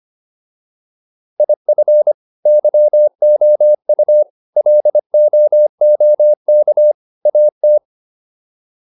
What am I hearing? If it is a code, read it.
Morse code, heard as a keyed tone, about 25 wpm, 600 Hz. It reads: IF YOU LOOK AT